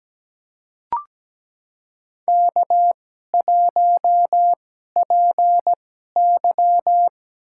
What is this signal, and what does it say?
Morse code, heard as a keyed tone, about 17 wpm, spaced as for 12 wpm, 700 Hz. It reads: K1PY
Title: USA Call Signs - 4 Characters - 2x Character Spacing - 17wpm